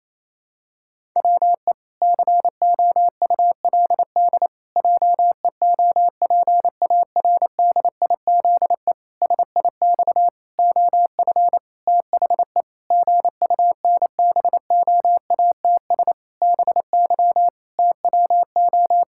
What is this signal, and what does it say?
Morse code, heard as a keyed tone, about 28 wpm, 700 Hz. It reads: WE COULD JEOPARDIZE SIX OF THE GUNBOATS BY TWO
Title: Pangram Challenge v2 - 28wpm